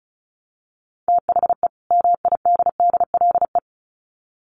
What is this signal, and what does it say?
Morse code, heard as a keyed tone, about 34 wpm, 700 Hz. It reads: THE MIDDLE
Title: Sets of 2 Words 35wpm